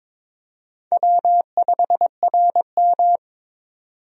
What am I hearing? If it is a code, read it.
Morse code, heard as a keyed tone, about 22 wpm, 700 Hz. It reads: W5RM
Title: USA Call Signs - 4 Characters - 22wpm